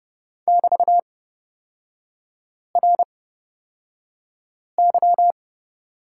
Morse code, heard as a keyed tone, about 30 wpm, spaced as for 5 wpm, 700 Hz. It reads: <BT>RY
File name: Lesson 36e - BT - Three Characters 30wpm